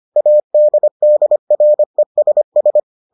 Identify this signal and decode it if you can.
Morse code, heard as a keyed tone, about 25 wpm, 600 Hz. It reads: ADDRESS